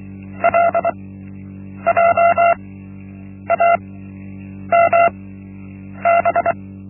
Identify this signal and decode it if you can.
Morse code, heard as a keyed tone, about 24 wpm, spaced as for 8 wpm, 680 Hz. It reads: LJAMB
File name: Linksmakalnis-1